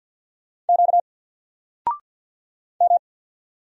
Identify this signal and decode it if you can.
Morse code, heard as a keyed tone, about 50 wpm, 700 Hz. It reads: Q M
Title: Single Letter-Number 50wpm